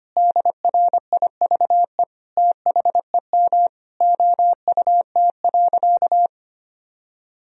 Morse code, heard as a keyed tone, about 25 wpm, 700 Hz. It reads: DRIVE THEM OUT.
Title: Sentences from Top 500 Words and Abbreviations 25wpm